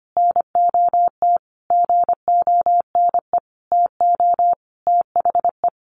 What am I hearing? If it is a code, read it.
Morse code, heard as a keyed tone, about 25 wpm, 700 Hz. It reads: NOT GONE TO THE
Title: A Sound Of Thunder 25wpm